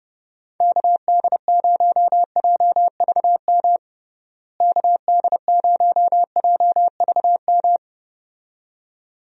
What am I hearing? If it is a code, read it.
Morse code, heard as a keyed tone, about 30 wpm, 700 Hz. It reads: KD0JVM KD0JVM
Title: USA Call Signs - 6 Characters - Repeated 2x - 30wpm